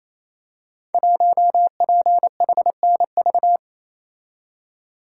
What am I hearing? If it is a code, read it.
Morse code, heard as a keyed tone, about 29 wpm, 700 Hz. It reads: 1PHNV
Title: Code Groups - 5 Letter:Numbers - 28wpm